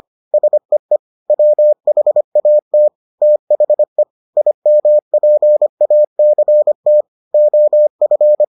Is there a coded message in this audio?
Yes: Morse code from 0.3 to 8.4 s, heard as a keyed tone, about 25 wpm, 600 Hz. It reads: SEE WHAT THE IMPACT OF